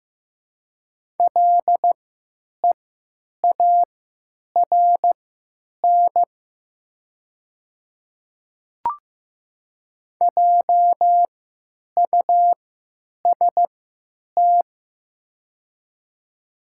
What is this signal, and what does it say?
Morse code, heard as a keyed tone, about 16 wpm, spaced as for 9 wpm, 700 Hz. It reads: LEARN JUST
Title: 3x Character Spacing - Top 300 Words - 15wpm